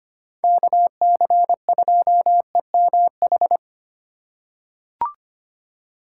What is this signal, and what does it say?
Morse code, heard as a keyed tone, about 25 wpm, 700 Hz. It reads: KC2EMH
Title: USA Call Signs - 6 Characters - 3x Character Spacing - 25wpm